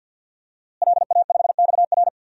Morse code, heard as a keyed tone, about 47 wpm, 700 Hz. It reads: LA5XR
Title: DXCC-100 Call Signs - 5 Characters - Repeated 3x - 50wpm